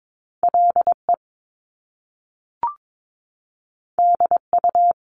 Morse code, heard as a keyed tone, about 22 wpm, 700 Hz. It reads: LE DU
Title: Top Two Letter Combinations in Words 22wpm